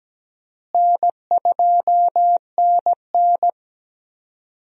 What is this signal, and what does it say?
Morse code, heard as a keyed tone, about 17 wpm, 700 Hz. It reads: N2NN
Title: USA Call Signs - 4 Characters - Repeated 3x - 17wpm